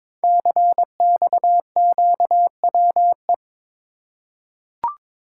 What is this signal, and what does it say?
Morse code, heard as a keyed tone, about 22 wpm, 700 Hz. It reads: CXQWE
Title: Code Groups - 5 Letters - 22wpm